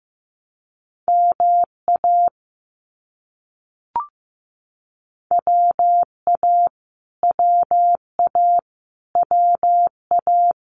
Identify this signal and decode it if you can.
Morse code, heard as a keyed tone, about 15 wpm, 700 Hz. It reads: MA WA WA WA
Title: US States Repeated 3x 15wpm